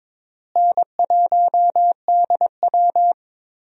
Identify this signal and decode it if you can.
Morse code, heard as a keyed tone, about 22 wpm, 700 Hz. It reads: N1DW
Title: USA Call Signs - 4 Characters - Repeated 2x - 22wpm